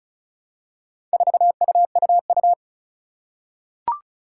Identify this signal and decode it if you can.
Morse code, heard as a keyed tone, about 36 wpm, 700 Hz. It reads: 4UUU
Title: Four Challenging Letter-Number Combinations 35wpm